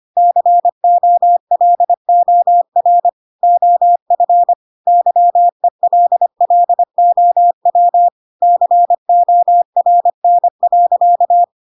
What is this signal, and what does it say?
Morse code, heard as a keyed tone, about 25 wpm, 700 Hz. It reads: COLOR OF YELLOW CORN.